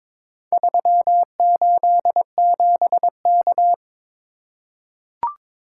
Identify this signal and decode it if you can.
Morse code, heard as a keyed tone, about 22 wpm, 700 Hz. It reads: 387K
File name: practice / 22wpm / Four Letter-Number Combinations - 22wpm